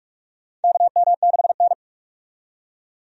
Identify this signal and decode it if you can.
Morse code, heard as a keyed tone, about 46 wpm, 700 Hz. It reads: KM6N
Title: USA Call Signs - 4 Characters - 45wpm